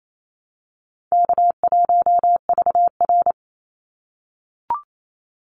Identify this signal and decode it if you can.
Morse code, heard as a keyed tone, about 28 wpm, 700 Hz. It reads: K1VR
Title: Easy Call Signs - 28wpm